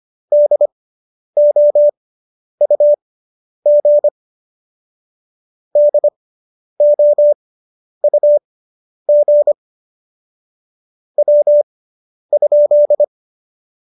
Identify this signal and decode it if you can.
Morse code, heard as a keyed tone, about 25 wpm, spaced as for 10 wpm, 590 Hz. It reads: DOUG DOUG W?
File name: QSO220 10